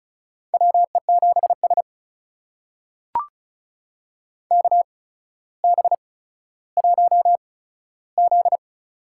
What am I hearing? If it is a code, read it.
Morse code, heard as a keyed tone, about 36 wpm, spaced as for 15 wpm, 700 Hz. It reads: WE7S KB1Z
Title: USA Call Signs - 4 Characters - 8x Character Spacing - 35wpm